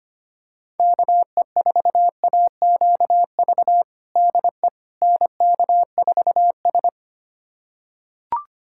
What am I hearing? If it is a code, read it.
Morse code, heard as a keyed tone, about 25 wpm, 700 Hz. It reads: KE4AQV DE NK4S
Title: USA Callsign DE Callsign 25wpm